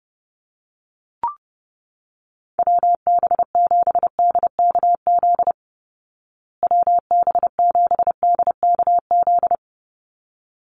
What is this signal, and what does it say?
Morse code, heard as a keyed tone, about 30 wpm, 700 Hz. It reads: WB7DKZ WB7DKZ
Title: Call Signs Repeated 3x - 30wpm